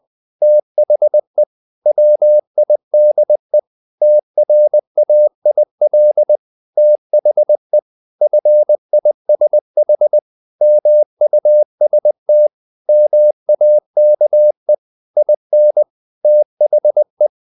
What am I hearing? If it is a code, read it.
Morse code, heard as a keyed tone, about 20 wpm, 600 Hz. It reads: THE WIDE TRAIL THE FISH MUST MAKE IN THE
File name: the old man and the sea part 2 20wpm